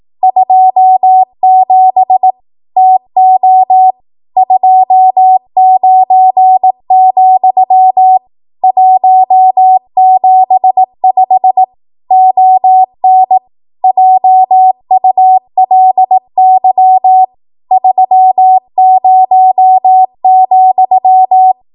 Morse code, heard as a keyed tone, about 18 wpm, 750 Hz. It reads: 27 TO 29, 175 ON JULY 30,
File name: Archive / 18 WPM / 18WPM